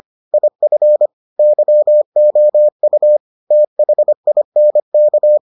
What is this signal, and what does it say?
Morse code, heard as a keyed tone, about 25 wpm, 600 Hz. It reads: IF YOU THINK